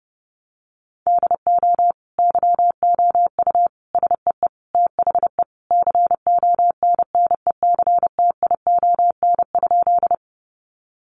Morse code, heard as a keyed tone, about 30 wpm, 700 Hz. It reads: DO YOU SEE THE CONNECTION?
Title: Sentences from Top 1300 Words 30wpm